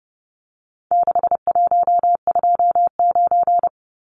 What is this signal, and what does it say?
Morse code, heard as a keyed tone, about 30 wpm, 700 Hz. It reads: 6129
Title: Numbers - 4-Digit 30wpm